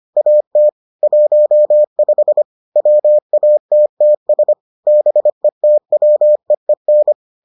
Morse code, heard as a keyed tone, about 25 wpm, 600 Hz. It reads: AT 15 WATTS BETWEEN